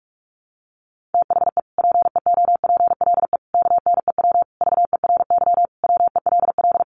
Morse code, heard as a keyed tone, about 44 wpm, 700 Hz. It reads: THE PEOPLE KNEW VERY WELL
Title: Sentences from Top 1100 Words 45wpm